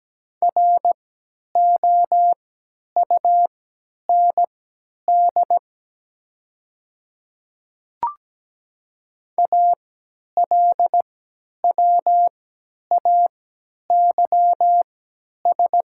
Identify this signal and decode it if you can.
Morse code, heard as a keyed tone, about 17 wpm, spaced as for 10 wpm, 700 Hz. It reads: ROUND ALWAYS
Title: 3x Character Spacing - Top 300 Words - 17wpm